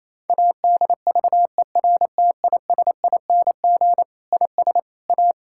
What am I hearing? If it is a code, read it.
Morse code, heard as a keyed tone, about 28 wpm, 700 Hz. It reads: ADVERTISING IS A
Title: Sentences from Top 1000 Words - Review Part 6 - 28wpm